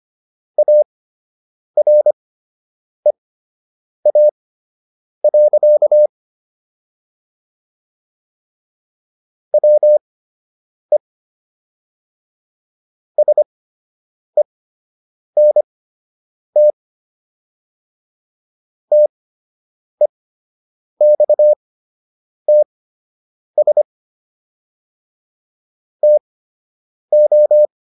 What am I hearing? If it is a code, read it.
Morse code, heard as a keyed tone, about 25 wpm, spaced as for 8 wpm, 600 Hz. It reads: AREA. WE SENT TEXTS TO